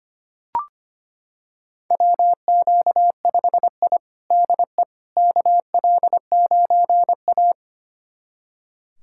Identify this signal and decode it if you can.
Morse code, heard as a keyed tone, about 25 wpm, 700 Hz. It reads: WQ5I DE KL9A
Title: USA Callsign DE Callsign - 4 Characters - 25wpm